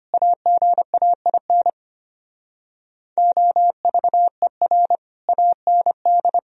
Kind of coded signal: Morse code

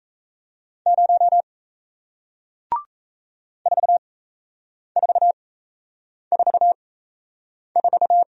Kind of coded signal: Morse code

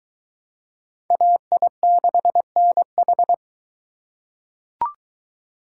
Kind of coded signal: Morse code

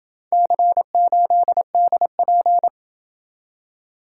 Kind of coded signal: Morse code

